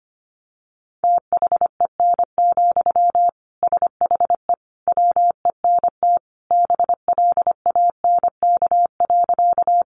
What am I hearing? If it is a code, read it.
Morse code, heard as a keyed tone, about 25 wpm, 700 Hz. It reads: THEN, SHE WENT BLANK.
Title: Sentences from Top 1500 Words 25wpm